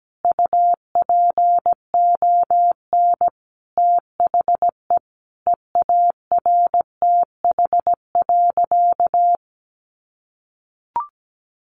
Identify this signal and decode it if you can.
Morse code, heard as a keyed tone, about 17 wpm, 700 Hz. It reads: UPON THE EARTH.
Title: Sentences from Top 1200 Words 17wpm